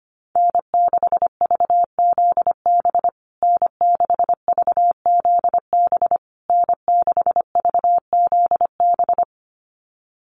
Morse code, heard as a keyed tone, about 25 wpm, 700 Hz. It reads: N6VZB N6VZB N6VZB
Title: USA Call Signs Repeated 3x 25wpm